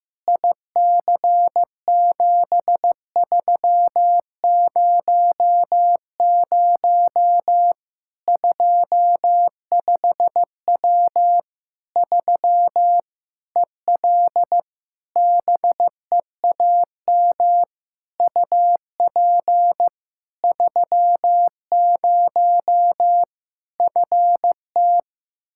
Morse code, heard as a keyed tone, about 15 wpm, 700 Hz. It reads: IC7300 25W 3 EL BEAM UP 30 FT
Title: Practice QSO - Exchange 2 - Info - 15wpm